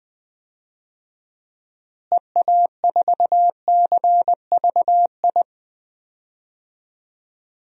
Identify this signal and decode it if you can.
Morse code, heard as a keyed tone, about 21 wpm, 700 Hz. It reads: EA4CVI
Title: DXCC-100 Call Signs - 6 Characters - Repeated 2x - 20wpm